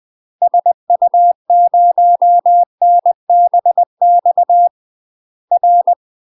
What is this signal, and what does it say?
Morse code, heard as a keyed tone, about 20 wpm, 700 Hz. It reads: SU0NBX R
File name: UK MASTER QSOs TRAINING / 20WPM / UK MASTER QSOs 0058